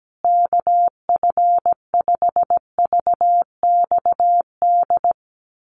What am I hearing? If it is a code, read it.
Morse code, heard as a keyed tone, about 17 wpm, 700 Hz. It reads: KF5VXD
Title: USA Call Signs Repeated 3x 17wpm